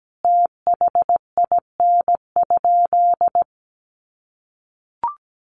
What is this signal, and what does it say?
Morse code, heard as a keyed tone, about 17 wpm, 700 Hz. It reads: THIN?
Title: Sentences from Top 1200 Words 17wpm